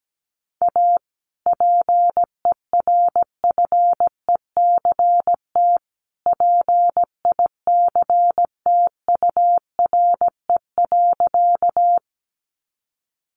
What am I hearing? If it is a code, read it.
Morse code, heard as a keyed tone, about 17 wpm, 700 Hz. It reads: A PERFECT PICTURE.
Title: Sentences from Top 1000 Words 17wpm